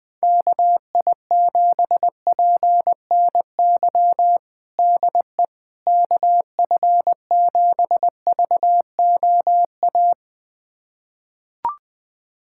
Morse code, heard as a keyed tone, about 20 wpm, 700 Hz. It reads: KI7PNY DE KF7VOA
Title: USA Callsign DE Callsign - 6 Characters - 20wpm